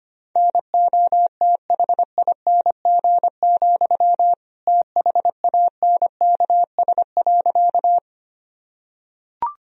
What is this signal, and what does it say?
Morse code, heard as a keyed tone, about 25 wpm, 700 Hz. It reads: NOTHING, THANKS.
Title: Sentences from Top 1000 Words - Review Part 3 - 25wpm